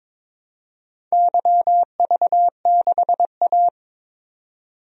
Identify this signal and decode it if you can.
Morse code, heard as a keyed tone, about 23 wpm, 700 Hz. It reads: YV6A
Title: DXCC-100 Call Signs - 4 Characters - 22wpm